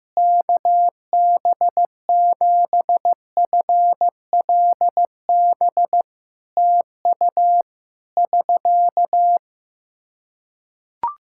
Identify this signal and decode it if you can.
Morse code, heard as a keyed tone, about 15 wpm, 700 Hz. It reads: KB7FLB TU <SK>
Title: Practice QSO - Exchange 4 - Ending - 15wpm